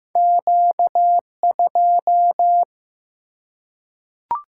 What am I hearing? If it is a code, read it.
Morse code, heard as a keyed tone, about 15 wpm, 700 Hz. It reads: Q2
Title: Single-Letter Single-Number 15wpm